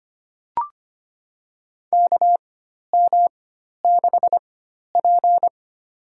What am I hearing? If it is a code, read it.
Morse code, heard as a keyed tone, about 25 wpm, spaced as for 12 wpm, 700 Hz. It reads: KM6P
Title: USA Call Signs - 4x Character Spacing - 25wpm